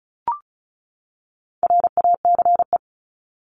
Morse code, heard as a keyed tone, about 34 wpm, 700 Hz. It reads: RACE
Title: Top 1501-1750 Words 35wpm